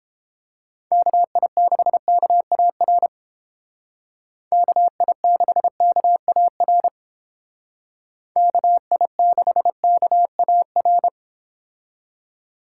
Morse code, heard as a keyed tone, about 29 wpm, 700 Hz. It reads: KI6KAR KI6KAR KI6KAR